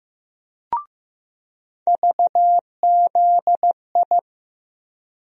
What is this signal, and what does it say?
Morse code, heard as a keyed tone, about 15 wpm, 700 Hz. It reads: VZI
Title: USA Call Sign Postfixes - 3 Letters 15wpm